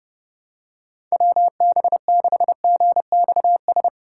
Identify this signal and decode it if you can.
Morse code, heard as a keyed tone, about 30 wpm, 700 Hz. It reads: WB6GXS
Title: USA Call Signs - 6 Characters - 3x Character Spacing - 30wpm